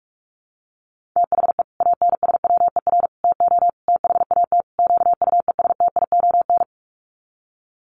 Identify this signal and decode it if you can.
Morse code, heard as a keyed tone, about 44 wpm, 700 Hz. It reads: THE ANSWER TO THAT QUESTION
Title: Sets of 5 Words Encore2 45wpm